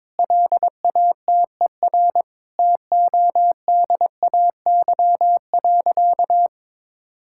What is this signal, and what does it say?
Morse code, heard as a keyed tone, about 22 wpm, 700 Hz. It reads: LATER TODAY.
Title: Sentences from Top 1600 Words 22wpm